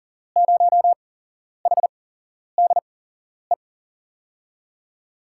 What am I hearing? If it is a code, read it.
Morse code, heard as a keyed tone, about 40 wpm, 700 Hz. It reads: 0 H D E
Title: USA Call Signs - 6 Characters - 8x Character Spacing - 40wpm